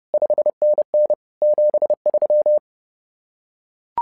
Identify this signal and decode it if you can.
Morse code, heard as a keyed tone, about 30 wpm, 600 Hz. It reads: 5NN 73
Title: SOTA QSO Practice 30wpm